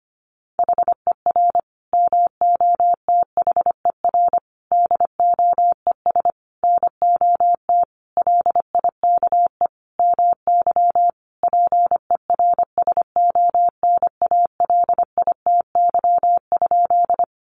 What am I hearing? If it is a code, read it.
Morse code, heard as a keyed tone, about 25 wpm, 700 Hz. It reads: HER MOTHER DOES NOT LIKE MY PERSONALITY?
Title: Sentences from Top 1400 Words 25wpm